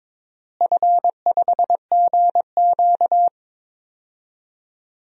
Morse code, heard as a keyed tone, about 22 wpm, 700 Hz. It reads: F5GQ
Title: DXCC-100 Call Signs - 4 Characters - Repeated 3x - 22wpm